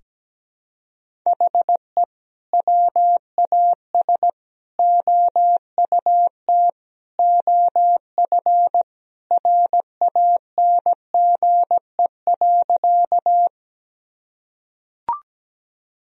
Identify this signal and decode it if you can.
Morse code, heard as a keyed tone, about 17 wpm, 700 Hz. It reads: HE WAS OUT OF RANGE.
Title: Sentences from Top 1000 Words - Review Part 4 - 17wpm